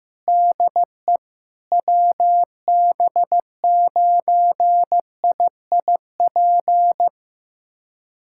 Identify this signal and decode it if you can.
Morse code, heard as a keyed tone, about 15 wpm, 700 Hz. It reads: DE WB9IIP
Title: USA Callsign DE Callsign - 6 Characters - 15wpm